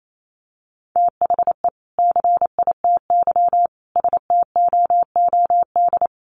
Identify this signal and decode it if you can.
Morse code, heard as a keyed tone, about 28 wpm, 700 Hz. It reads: THE CITY STOOD
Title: Sentences from Top 1100 Words 28wpm